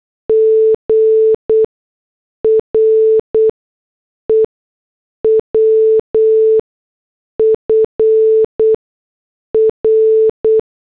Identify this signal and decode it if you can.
Morse code, heard as a keyed tone, about 8 wpm, spaced as for 6 wpm, 440 Hz. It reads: GREW FR